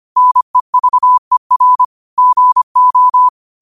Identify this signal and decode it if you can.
Morse code, heard as a keyed tone, about 25 wpm, 1000 Hz. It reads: NEVER GO
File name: qotdfiles / 25wpm